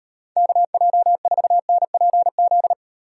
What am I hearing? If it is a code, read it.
Morse code, heard as a keyed tone, about 35 wpm, 700 Hz. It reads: KJ4NPZ